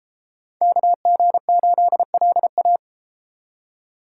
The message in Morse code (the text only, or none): KG8LA